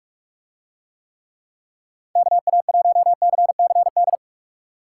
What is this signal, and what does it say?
Morse code, heard as a keyed tone, about 45 wpm, 700 Hz. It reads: KA1CCD